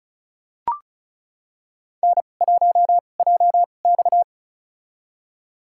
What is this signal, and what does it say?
Morse code, heard as a keyed tone, about 36 wpm, spaced as for 25 wpm, 700 Hz. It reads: N1JX
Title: USA Call Signs - 2x Character Spacing - 35wpm